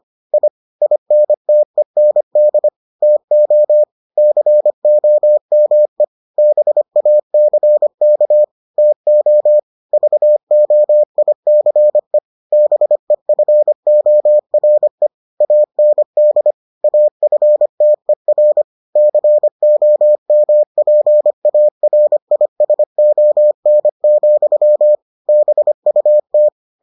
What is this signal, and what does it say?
Morse code, heard as a keyed tone, about 25 wpm, 600 Hz. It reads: I INTEND TO COME BACK TO VOICE BEFORE AND AFTER COMPARISON, BUT